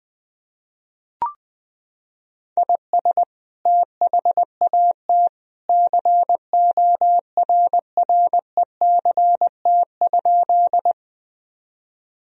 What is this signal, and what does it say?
Morse code, heard as a keyed tone, about 20 wpm, 700 Hz. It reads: IS THAT CORRECT?
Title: Sentences from Top 1000 Words - Review Part 2 - 20wpm